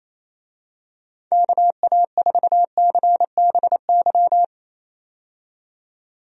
Morse code, heard as a keyed tone, about 27 wpm, 700 Hz. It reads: KA4CBY
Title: USA Call Signs - 6 Characters - 28wpm